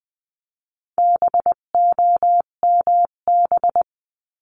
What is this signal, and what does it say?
Morse code, heard as a keyed tone, about 20 wpm, 700 Hz. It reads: BOMB
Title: 258 Lesson 42c - Words 16-20wpm